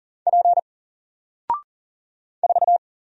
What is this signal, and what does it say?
Morse code, heard as a keyed tone, about 40 wpm, 700 Hz. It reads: P 4
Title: Single Character (Letter:Number:Punctuation:Prosign) Alt 40wpm